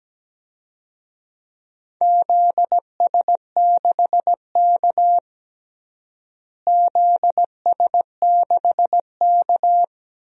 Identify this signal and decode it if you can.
Morse code, heard as a keyed tone, about 17 wpm, 700 Hz. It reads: ZS6K ZS6K
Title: DXCC-100 Call Signs - 4 Characters - Repeated 2x - 17wpm